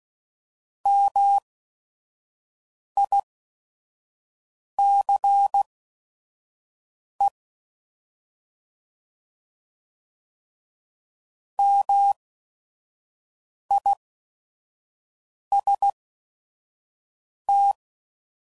Morse code, heard as a keyed tone, about 16 wpm, spaced as for 5 wpm, 800 Hz. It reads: MICE MIST